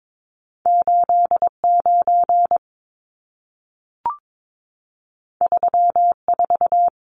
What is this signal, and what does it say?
Morse code, heard as a keyed tone, about 22 wpm, 700 Hz. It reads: 89 34
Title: Numbers - 2-Digit 22wpm